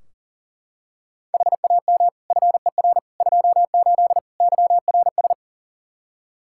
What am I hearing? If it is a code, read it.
Morse code, heard as a keyed tone, about 40 wpm, 700 Hz. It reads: HAM FER 28 YRS
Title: K1USN SST QSO Practice 40wpm